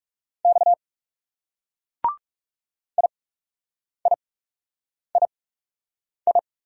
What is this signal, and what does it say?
Morse code, heard as a keyed tone, about 38 wpm, spaced as for 8 wpm, 700 Hz. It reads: X IIII